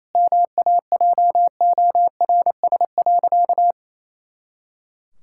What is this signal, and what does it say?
Morse code, heard as a keyed tone, about 28 wpm, 700 Hz. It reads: MAJORS.
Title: Pangram Challenge v2 - 28wpm